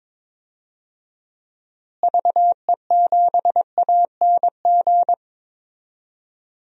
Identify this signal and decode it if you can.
Morse code, heard as a keyed tone, about 21 wpm, 700 Hz. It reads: VE7ANG
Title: DXCC-100 Call Signs - 6 Characters - Repeated 3x - 22wpm